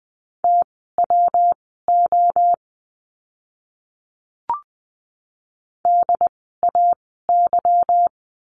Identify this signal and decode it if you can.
Morse code, heard as a keyed tone, about 20 wpm, spaced as for 15 wpm, 700 Hz. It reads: TWO DAY
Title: 206 Lesson 33f - Letter j - Words 20wpm